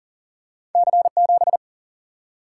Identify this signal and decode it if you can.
Morse code, heard as a keyed tone, about 40 wpm, 700 Hz. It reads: C7
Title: Single-Letter Single-Number 40wpm